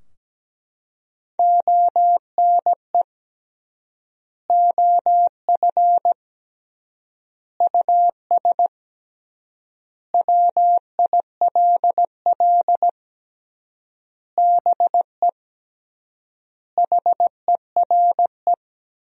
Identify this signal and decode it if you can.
Morse code, heard as a keyed tone, about 17 wpm, 700 Hz. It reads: ONE OF US WILL BE HERE
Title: 3x Word-Spacing - Sentences from Top 300 Words - 17wpm